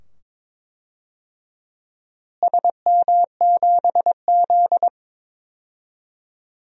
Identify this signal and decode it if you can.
Morse code, heard as a keyed tone, about 22 wpm, 700 Hz. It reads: SM7Z
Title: DXCC-100 Call Signs - 4 Characters - Repeated 2x - 22wpm